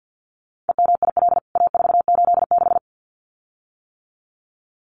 Morse code, heard as a keyed tone, about 49 wpm, 700 Hz. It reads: EMIL A486
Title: CWT Exchange 50wpm